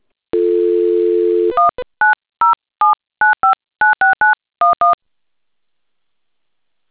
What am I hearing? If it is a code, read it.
DTMF, heard as telephone touch tones, tones of about 150 ms. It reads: 19079596911